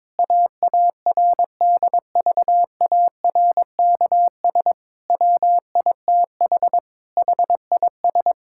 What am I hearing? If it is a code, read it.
Morse code, heard as a keyed tone, about 22 wpm, 700 Hz. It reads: AARDVARKS WITH HIS
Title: Pangram Challenge v2 - 22wpm